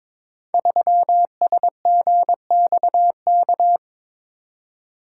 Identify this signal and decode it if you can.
Morse code, heard as a keyed tone, about 22 wpm, 700 Hz. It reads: 3SGXK
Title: Code Groups - 5 Letter:Numbers - 22wpm